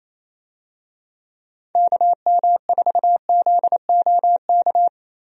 Morse code, heard as a keyed tone, about 27 wpm, 700 Hz. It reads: KM4ZOK
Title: USA Call Signs - 6 Characters - 28wpm